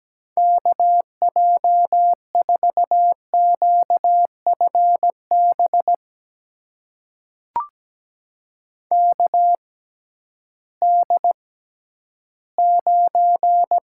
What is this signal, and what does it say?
Morse code, heard as a keyed tone, about 17 wpm, spaced as for 11 wpm, 700 Hz. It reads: KJ4QFB KD9
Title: USA Call Signs - 6 Characters - 6x Character Spacing - 17wpm